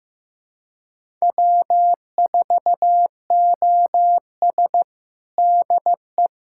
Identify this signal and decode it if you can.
Morse code, heard as a keyed tone, about 14 wpm, 700 Hz. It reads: W4OS DE
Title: USA Callsign DE Callsign 15wpm